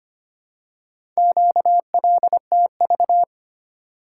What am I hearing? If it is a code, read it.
Morse code, heard as a keyed tone, about 25 wpm, 700 Hz. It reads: QLTV
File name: practice / 25wpm / Four Letter-Number Combinations - 25wpm